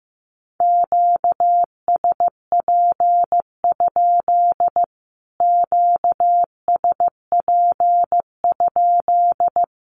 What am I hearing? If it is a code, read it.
Morse code, heard as a keyed tone, about 15 wpm, 700 Hz. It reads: QSP? QSP?
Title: Q-Codes Repeated 3x 15wpm